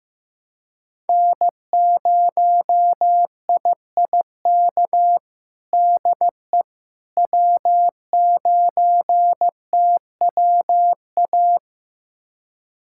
Morse code, heard as a keyed tone, about 15 wpm, 700 Hz. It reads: N0IIK DE W9TWA
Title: USA Callsign DE Callsign - 5 Characters - 15wpm